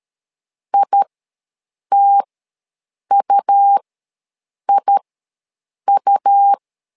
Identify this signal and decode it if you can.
Morse code, heard as a keyed tone, about 13 wpm, spaced as for 7 wpm, 800 Hz. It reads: ITUIU
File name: LEZIONI / 2 UHDN / 2 U ws 12wpm 1655